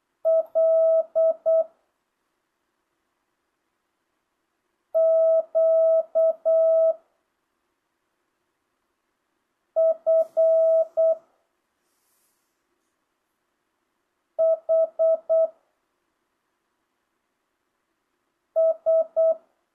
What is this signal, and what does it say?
Morse code, heard as a keyed tone, about 8 wpm, 660 Hz. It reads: L Q F H S
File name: jh8cbh / 2025cw